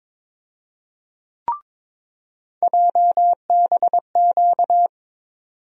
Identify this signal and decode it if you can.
Morse code, heard as a keyed tone, about 22 wpm, 700 Hz. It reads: JBQ
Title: USA Call Sign Postfixes - 3 Letters 22wpm